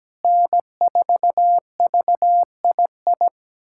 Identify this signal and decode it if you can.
Morse code, heard as a keyed tone, about 17 wpm, 700 Hz. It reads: N4VII
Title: USA Callsign DE Callsign 17wpm